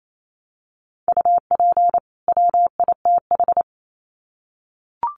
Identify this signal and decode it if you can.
Morse code, heard as a keyed tone, about 28 wpm, 700 Hz. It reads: UP WITH
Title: Sets of 2 Words 28wpm